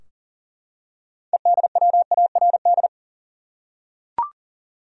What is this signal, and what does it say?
Morse code, heard as a keyed tone, about 38 wpm, 700 Hz. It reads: EDWARD
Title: USA Common Names - 40wpm